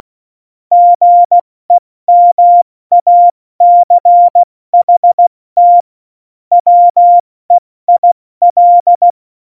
Morse code, heard as a keyed tone, about 16 wpm, 700 Hz. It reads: GEMACHT WEIL
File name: Morsekurs / Witze / Witz 08. Januar 2019, 70 F80